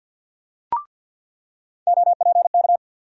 Morse code, heard as a keyed tone, about 47 wpm, 700 Hz. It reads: OPK
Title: USA Call Sign Postfixes - 3 Letters 50wpm